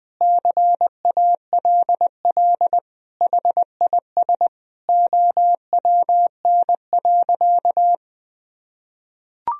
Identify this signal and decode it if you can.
Morse code, heard as a keyed tone, about 21 wpm, 700 Hz. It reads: CALL HIS OWN.
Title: Sentences from Top 1000 Words - Review Part 2 - 20wpm